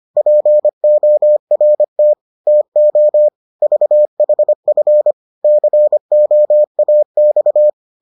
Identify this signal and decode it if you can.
Morse code, heard as a keyed tone, about 25 wpm, 600 Hz. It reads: PORT TO VHF COAX